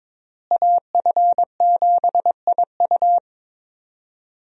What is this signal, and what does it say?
Morse code, heard as a keyed tone, about 22 wpm, 700 Hz. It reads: AF7IU
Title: USA Call Signs - 5 Characters - Repeated 2x - 22wpm